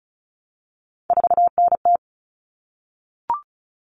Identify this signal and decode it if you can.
Morse code, heard as a keyed tone, about 34 wpm, 700 Hz. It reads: 4NT
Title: Common Cut Numbers - 3-Digit 35wpm